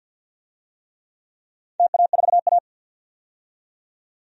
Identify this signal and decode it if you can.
Morse code, heard as a keyed tone, about 47 wpm, 700 Hz. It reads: TA4A
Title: DXCC-100 Call Signs - 4 Characters - 50wpm